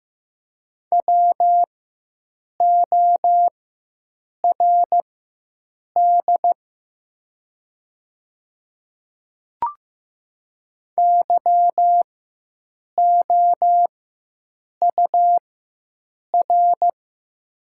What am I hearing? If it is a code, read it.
Morse code, heard as a keyed tone, about 15 wpm, spaced as for 7 wpm, 700 Hz. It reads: WORD YOUR
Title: 4x Character Spacing - Top 100 Words - 15wpm